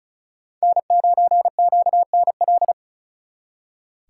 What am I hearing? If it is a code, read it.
Morse code, heard as a keyed tone, about 46 wpm, 700 Hz. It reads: N9QNL